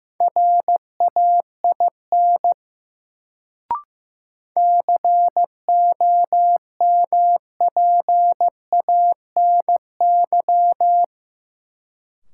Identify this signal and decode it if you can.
Morse code, heard as a keyed tone, about 15 wpm, 700 Hz. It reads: RAIN COMPANY
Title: Sentences from Top 600 Words - New Word Drill - 15wpm